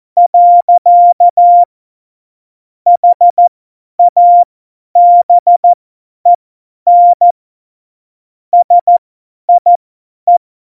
Morse code, heard as a keyed tone, about 14 wpm, spaced as for 10 wpm, 700 Hz. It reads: . HABEN SIE
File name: Morsekurs / Witze / Witz 06 March 2018, 50 F70